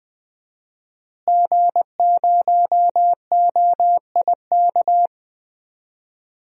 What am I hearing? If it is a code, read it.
Morse code, heard as a keyed tone, about 20 wpm, 700 Hz. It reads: G0OIK
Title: DXCC-100 Call Signs - 5 Characters - Repeated 2x - 20wpm